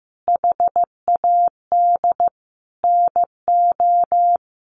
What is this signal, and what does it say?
Morse code, heard as a keyed tone, about 15 wpm, 700 Hz. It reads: HAD NO
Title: Sentences from Top 1500 Words 15wpm